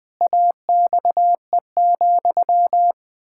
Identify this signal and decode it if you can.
Morse code, heard as a keyed tone, about 20 wpm, 700 Hz. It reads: AXE,
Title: Pangram Challenge v2 - 20wpm